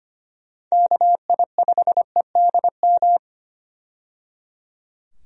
Morse code, heard as a keyed tone, about 25 wpm, 700 Hz. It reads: KI5EDM
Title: USA Call Signs - 6 Characters - 25wpm